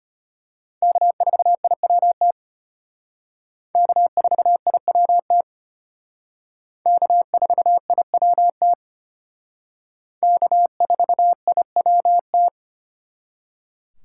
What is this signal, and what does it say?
Morse code, heard as a keyed tone, about 33 wpm, 700 Hz. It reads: K4IWT K4IWT K4IWT K4IWT